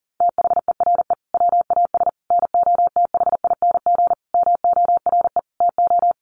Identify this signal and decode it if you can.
Morse code, heard as a keyed tone, about 40 wpm, 700 Hz. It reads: THERE WAS NOTHING MORE TO
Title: Sentences from Top 1100 Words 40wpm